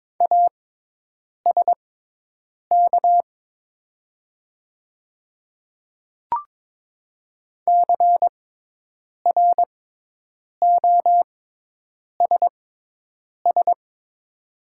Morse code, heard as a keyed tone, about 22 wpm, spaced as for 8 wpm, 700 Hz. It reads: ASK CROSS
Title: 6x Character Spacing - Top 300 Words - 22wpm